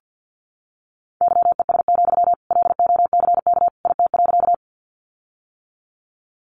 Morse code, heard as a keyed tone, about 50 wpm, 700 Hz. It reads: YES, LOCK IT.
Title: Sentences from Top 1300 Words 50wpm